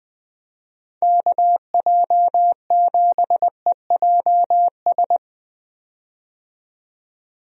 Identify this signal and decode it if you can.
Morse code, heard as a keyed tone, about 21 wpm, 700 Hz. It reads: KJ7EJS
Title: USA Call Signs - 6 Characters - 20wpm